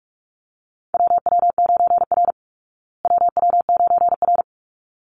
Morse code, heard as a keyed tone, about 44 wpm, 700 Hz. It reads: WW9R WW9R
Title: Call Signs Repeated 3x - 45wpm